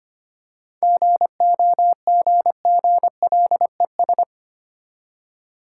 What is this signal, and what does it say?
Morse code, heard as a keyed tone, about 25 wpm, 690 Hz. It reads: GOGGLES
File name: practice / 25wpm / Easy Words with Hints - 25wpm